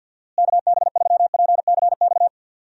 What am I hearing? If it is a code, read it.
Morse code, heard as a keyed tone, about 52 wpm, 700 Hz. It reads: KB3PCX
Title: USA Call Signs - 6 Characters - Repeated 3x - 50wpm